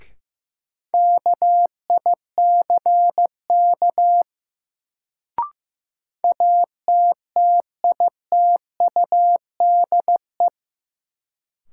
Morse code, heard as a keyed tone, about 15 wpm, 700 Hz. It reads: KICK ATTITUDE
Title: Sentences from Top 1300 Words - New Word Drill - 15wpm